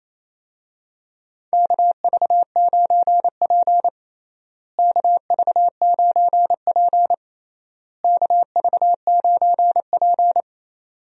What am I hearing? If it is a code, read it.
Morse code, heard as a keyed tone, about 28 wpm, 700 Hz. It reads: KV9P KV9P KV9P
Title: USA Call Signs - 4 Characters - Repeated 3x - 28wpm